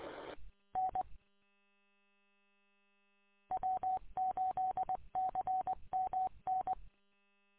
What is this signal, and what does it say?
Morse code, heard as a keyed tone, about 19 wpm, 750 Hz. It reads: N W8CMN
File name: w8cmn- -combined